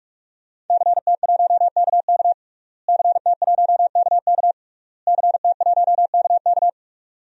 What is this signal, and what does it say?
Morse code, heard as a keyed tone, about 46 wpm, 700 Hz. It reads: CT1KK CT1KK CT1KK
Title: DXCC-100 Call Signs - 5 Characters - Repeated 3x - 45wpm